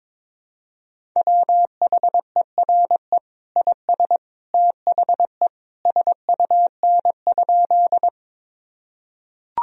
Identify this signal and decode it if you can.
Morse code, heard as a keyed tone, about 22 wpm, 700 Hz. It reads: WHERE IS THE SUN?
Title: Sentences from Top 1000 Words - Review Part 1 - 22wpm